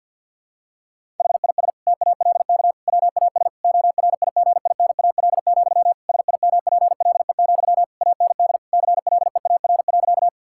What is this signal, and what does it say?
Morse code, heard as a keyed tone, about 52 wpm, 700 Hz. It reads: HIS TALK WAS ORIGINAL, SIMPLE, AND CLEAR.
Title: Sentences from Top 1000 Words - Review Part 6 - 50wpm